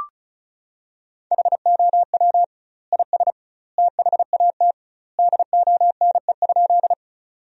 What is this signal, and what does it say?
Morse code, heard as a keyed tone, about 34 wpm, 700 Hz. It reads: HOW IS THAT DONE?
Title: 2x Word-Spacing - Sentences from Top 500 Words - 35wpm